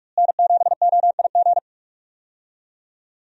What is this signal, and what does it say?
Morse code, heard as a keyed tone, about 41 wpm, 700 Hz. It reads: N7OIG
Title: USA Call Signs - 5 Characters - Repeated 3x - 45wpm